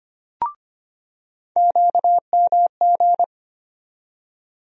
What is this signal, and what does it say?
Morse code, heard as a keyed tone, about 25 wpm, 700 Hz. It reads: QMG
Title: USA Call Sign Postfixes - 3 Letters 25wpm